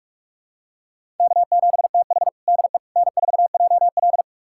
Morse code, heard as a keyed tone, about 43 wpm, 700 Hz. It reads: K7TH DE N4JL
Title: USA Callsign DE Callsign - 4 Characters - 45wpm